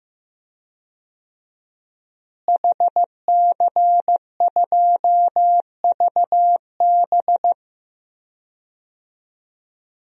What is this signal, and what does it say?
Morse code, heard as a keyed tone, about 15 wpm, 700 Hz. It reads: HC2VB
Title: DXCC-100 Call Signs - 5 Characters - Repeated 2x - 15wpm